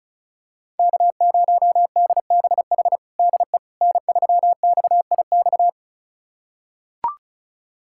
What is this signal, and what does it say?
Morse code, heard as a keyed tone, about 34 wpm, 700 Hz. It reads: K0DBH DE N3XIX
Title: USA Callsign DE Callsign - 5 Characters - 35wpm